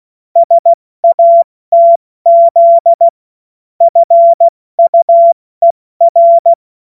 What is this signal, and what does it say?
Morse code, heard as a keyed tone, about 16 wpm, spaced as for 14 wpm, 680 Hz. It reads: SATZ FUER
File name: Morsekurs / Witze / Witz 06. Juli 2021, 70 F80